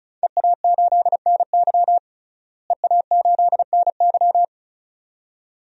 Morse code, heard as a keyed tone, about 34 wpm, 700 Hz. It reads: EA8NY EA8NY
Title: DXCC-100 Call Signs - 5 Characters - Repeated 2x - 35wpm